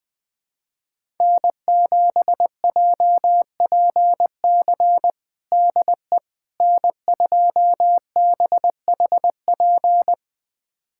Morse code, heard as a keyed tone, about 21 wpm, 700 Hz. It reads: N7JPC DE N2BHP
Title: USA Callsign DE Callsign - 5 Characters - 20wpm